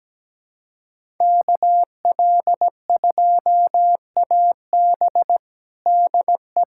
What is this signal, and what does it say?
Morse code, heard as a keyed tone, about 17 wpm, 700 Hz. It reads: KL2AB DE
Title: USA Callsign DE Callsign - 5 Characters - 17wpm